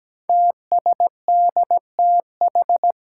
Morse code, heard as a keyed tone, about 17 wpm, 700 Hz. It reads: TSDTH
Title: Five Challenging Letter-Number Combinations 17wpm